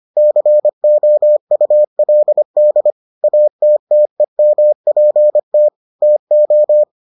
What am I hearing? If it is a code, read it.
Morse code, heard as a keyed tone, about 25 wpm, 600 Hz. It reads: COULD ATTEMPT TO